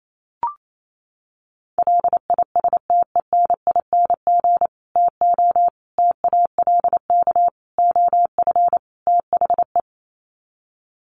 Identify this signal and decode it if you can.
Morse code, heard as a keyed tone, about 28 wpm, 700 Hz. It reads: LISTENING TO TALK OF THE
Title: Sets of 5 Words 28wpm